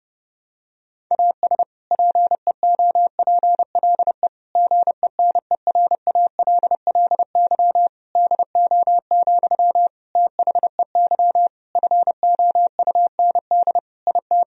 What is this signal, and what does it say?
Morse code, heard as a keyed tone, about 32 wpm, 700 Hz. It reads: AS PEOPLE GENERALLY DO, THEY FOUND IT
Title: Sentences from Top 1000 Words - Review Part 4 - 30wpm